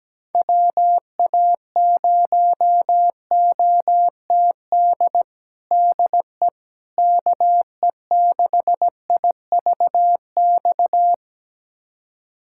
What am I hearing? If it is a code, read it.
Morse code, heard as a keyed tone, about 17 wpm, 700 Hz. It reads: WA0OTD DE KE6IVX
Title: USA Callsign DE Callsign 17wpm